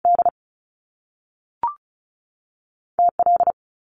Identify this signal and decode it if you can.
Morse code, heard as a keyed tone, about 36 wpm, 700 Hz. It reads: D T L
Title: Top Two Letter Combinations in Words 35wpm